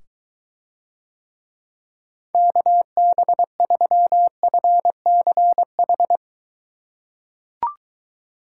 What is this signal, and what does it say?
Morse code, heard as a keyed tone, about 23 wpm, 700 Hz. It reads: KB3FCH